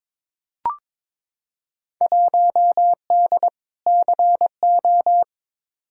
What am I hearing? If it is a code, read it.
Morse code, heard as a keyed tone, about 22 wpm, 700 Hz. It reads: 1D CO
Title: Field Day Exchanges - 22wpm